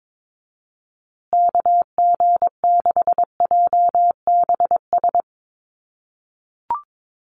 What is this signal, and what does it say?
Morse code, heard as a keyed tone, about 22 wpm, 700 Hz. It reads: KG6JBS
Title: USA Call Signs Repeated 3x 22wpm